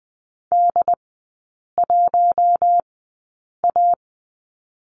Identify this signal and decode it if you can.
Morse code, heard as a keyed tone, about 20 wpm, spaced as for 8 wpm, 700 Hz. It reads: D1A
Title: 107 Lesson 18e - Letter m - Three Characters 20wpm